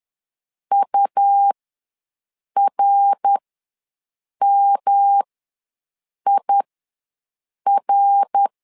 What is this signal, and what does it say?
Morse code, heard as a keyed tone, about 11 wpm, spaced as for 6 wpm, 800 Hz. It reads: URMIR
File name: LEZIONI / 3 MWRO / 3 MWR ws 10wpm 2151